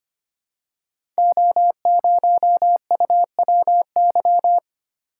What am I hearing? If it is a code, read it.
Morse code, heard as a keyed tone, about 25 wpm, 700 Hz. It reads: O0UWY
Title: Code Groups - 5 Letter:Numbers - 25wpm